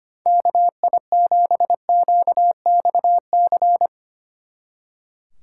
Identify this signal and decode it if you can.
Morse code, heard as a keyed tone, about 26 wpm, 700 Hz. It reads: KI7QXC
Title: Speed Racing - USA 2x3 Call Signs 25wpm